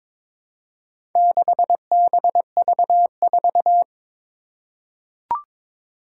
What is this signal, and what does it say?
Morse code, heard as a keyed tone, about 22 wpm, 700 Hz. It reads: 6BV4
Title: Four Challenging Letter-Number Combinations 22wpm